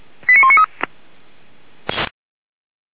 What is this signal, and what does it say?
Five-tone selective call (ZVEI), heard as quick runs of five tones, tones of about 70 ms. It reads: 89193